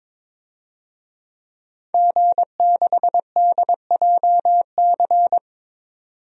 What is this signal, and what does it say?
Morse code, heard as a keyed tone, about 22 wpm, 700 Hz. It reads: G6DJC
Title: DXCC-100 Call Signs - 5 Characters - 22wpm